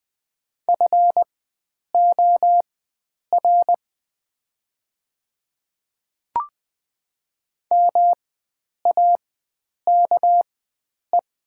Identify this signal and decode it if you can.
Morse code, heard as a keyed tone, about 20 wpm, spaced as for 9 wpm, 700 Hz. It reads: FOR MAKE
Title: 4x Character Spacing - Top 100 Words - 20wpm